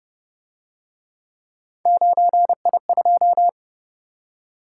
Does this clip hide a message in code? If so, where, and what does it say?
Yes: Morse code from 1.9 to 3.5 s, heard as a keyed tone, about 28 wpm, 700 Hz. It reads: 9I2